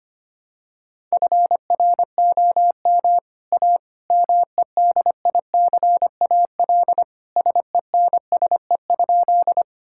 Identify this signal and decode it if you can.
Morse code, heard as a keyed tone, about 25 wpm, 700 Hz. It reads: FROM A MEDICAL SENSE?
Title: Sentences from Top 1000 Words - Review Part 5 - 25wpm